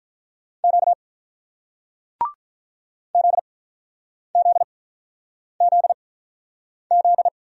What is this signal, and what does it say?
Morse code, heard as a keyed tone, about 47 wpm, 700 Hz. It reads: Q Z Z Z Z